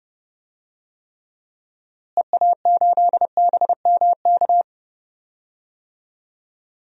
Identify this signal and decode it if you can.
Morse code, heard as a keyed tone, about 32 wpm, 700 Hz. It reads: EA8BMK
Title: DXCC-100 Call Signs - 6 Characters - Repeated 2x - 30wpm